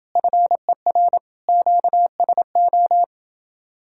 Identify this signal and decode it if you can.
Morse code, heard as a keyed tone, about 29 wpm, 700 Hz. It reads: FER QSO